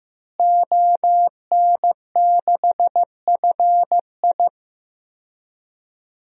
Morse code, heard as a keyed tone, about 15 wpm, 700 Hz. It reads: ON6FI
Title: DXCC-100 Call Signs - 5 Characters - Repeated 3x - 15wpm